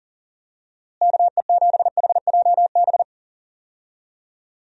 Morse code, heard as a keyed tone, about 40 wpm, 700 Hz. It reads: KE7HJB
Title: USA Call Signs - 6 Characters - Repeated 2x - 40wpm